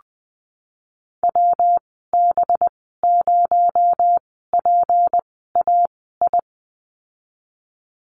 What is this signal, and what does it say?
Morse code, heard as a keyed tone, about 20 wpm, spaced as for 15 wpm, 700 Hz. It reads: WB0PAI
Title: 231 Lesson 37g - Letter x - Call Signs 20wpm